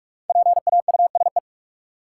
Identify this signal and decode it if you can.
Morse code, heard as a keyed tone, about 41 wpm, 700 Hz. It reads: PAUSE
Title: Five Letter Words - 45wpm